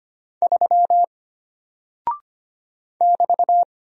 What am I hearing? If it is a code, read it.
Morse code, heard as a keyed tone, about 25 wpm, 700 Hz. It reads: 3 <BT>
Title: Single Character (Letter:Number:Punctuation:Prosign) Alt 25wpm